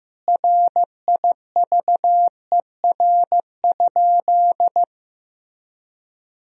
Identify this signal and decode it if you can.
Morse code, heard as a keyed tone, about 15 wpm, 700 Hz. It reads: RIVER?
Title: Sentences from Top 1000 Words - Review Part 4 - 15wpm